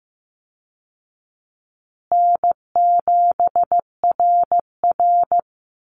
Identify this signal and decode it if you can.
Morse code, heard as a keyed tone, about 15 wpm, 700 Hz. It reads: N7RR
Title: Call Signs Repeated 2x - 15wpm